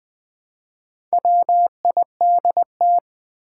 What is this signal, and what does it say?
Morse code, heard as a keyed tone, about 20 wpm, 700 Hz. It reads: WIDT
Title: Build A Word - 1 Letter At A Time - 3 to 5-Letter Words - 20wpm